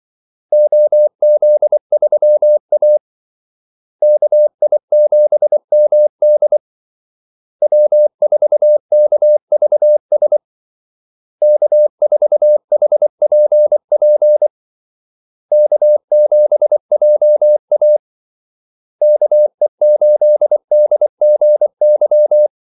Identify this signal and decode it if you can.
Morse code, heard as a keyed tone, about 24 wpm, 600 Hz. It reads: OZ3A KI7MD W4KVS K4HPP K7JA KE8DGY
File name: Callsigns-24wpm